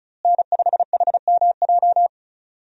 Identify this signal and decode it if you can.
Morse code, heard as a keyed tone, about 34 wpm, 700 Hz. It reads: N5HMJ
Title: USA Call Signs - 5 Characters - Repeated 3x - 35wpm